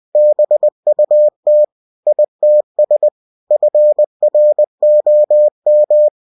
Morse code, heard as a keyed tone, about 20 wpm, 600 Hz. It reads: BUT ITS FROM
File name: cw-books / winnie-the-pooh / winnie-20wpm-page-0098